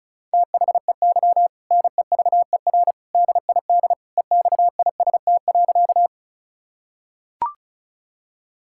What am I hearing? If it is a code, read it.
Morse code, heard as a keyed tone, about 36 wpm, 700 Hz. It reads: THEY NEVER DID EXIST.
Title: Sentences from Top 1000 Words - Review Part 5 - 35wpm